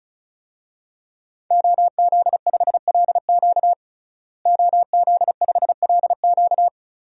Morse code, heard as a keyed tone, about 34 wpm, 700 Hz. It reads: OZ5LQ OZ5LQ
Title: DXCC-100 Call Signs - 5 Characters - Repeated 3x - 35wpm